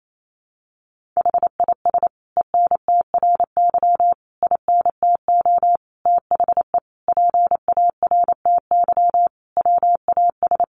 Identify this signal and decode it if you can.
Morse code, heard as a keyed tone, about 28 wpm, 700 Hz. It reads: HIS ENTRY INTO THE PARTY WAS
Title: Sentences from Top 1500 Words 28wpm